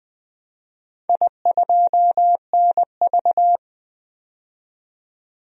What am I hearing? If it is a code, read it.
Morse code, heard as a keyed tone, about 20 wpm, 700 Hz. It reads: I2NV
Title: DXCC-100 Call Signs - 4 Characters - 20wpm